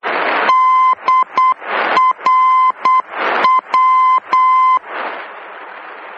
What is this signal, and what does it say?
Morse code, heard as a keyed tone, about 8 wpm, 1030 Hz. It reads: DRW
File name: germany / mx / DRW-D